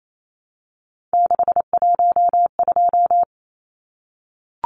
Morse code, heard as a keyed tone, about 28 wpm, 700 Hz. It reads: 612
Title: Numbers - 3-Digit 28wpm